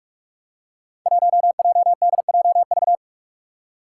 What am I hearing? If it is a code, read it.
Morse code, heard as a keyed tone, about 46 wpm, 700 Hz. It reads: 1JDJV
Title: Five Challenging Letter-Number Combinations 45wpm